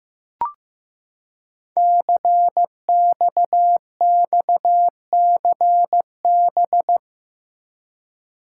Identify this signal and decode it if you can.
Morse code, heard as a keyed tone, about 15 wpm, 700 Hz. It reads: CXXCB
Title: Code Groups - 5 Letters - 15wpm